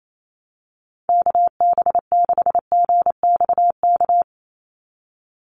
Morse code, heard as a keyed tone, about 28 wpm, 700 Hz. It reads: KB6GXK
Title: USA Call Signs 28wpm